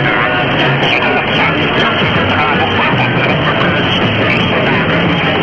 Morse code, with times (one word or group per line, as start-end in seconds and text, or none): none